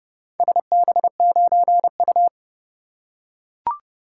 Morse code, heard as a keyed tone, about 30 wpm, 700 Hz. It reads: SB9U